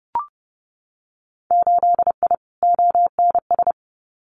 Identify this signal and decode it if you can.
Morse code, heard as a keyed tone, about 30 wpm, 700 Hz. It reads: 8I ONS
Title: Winter Field Day Exchanges Encore 30wpm